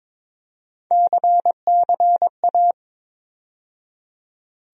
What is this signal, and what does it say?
Morse code, heard as a keyed tone, about 21 wpm, 700 Hz. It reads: CCA
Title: USA Call Sign Postfixes - 3 Letters 22wpm